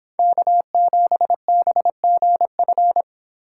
Morse code, heard as a keyed tone, about 26 wpm, 700 Hz. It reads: K7BGF